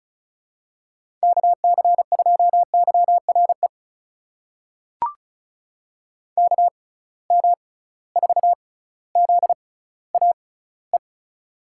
Morse code, heard as a keyed tone, about 36 wpm, spaced as for 18 wpm, 700 Hz. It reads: KC2YRE KM4ZAE
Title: USA Call Signs - 6 Characters - 6x Character Spacing - 35wpm